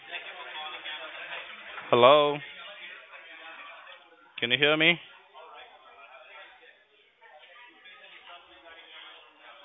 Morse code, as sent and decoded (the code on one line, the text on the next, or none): none